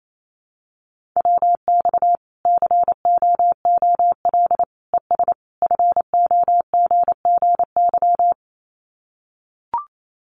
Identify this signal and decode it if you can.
Morse code, heard as a keyed tone, about 28 wpm, 700 Hz. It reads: WX COOL ES FOGGY
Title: Common QSO Phrases 28wpm